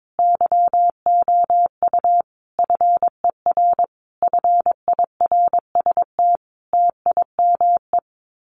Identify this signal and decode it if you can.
Morse code, heard as a keyed tone, about 22 wpm, 700 Hz. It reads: YOU FER FIRST TIME
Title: Common QSO Phrases 22wpm